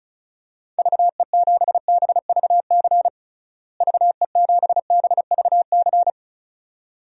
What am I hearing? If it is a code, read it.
Morse code, heard as a keyed tone, about 36 wpm, 700 Hz. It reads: VE7BVC VE7BVC
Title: DXCC-100 Call Signs - 6 Characters - Repeated 2x - 35wpm